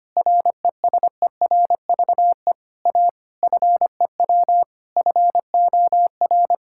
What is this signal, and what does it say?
Morse code, heard as a keyed tone, about 25 wpm, 700 Hz. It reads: RESERVE A FEW FOR
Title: Sentences from Top 1000 Words - Review Part 6 - 25wpm